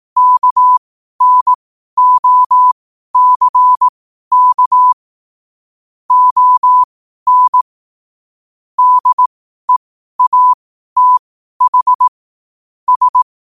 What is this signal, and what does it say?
Morse code, heard as a keyed tone, about 18 wpm, spaced as for 12 wpm, 1000 Hz. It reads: KNOCK ON DEATH S
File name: qotdfiles / 12wpm